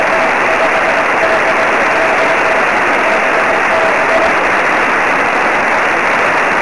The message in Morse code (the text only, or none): none